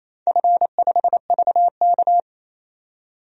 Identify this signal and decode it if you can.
Morse code, heard as a keyed tone, about 28 wpm, 700 Hz. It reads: F5VK
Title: DXCC-100 Call Signs - 4 Characters - 28wpm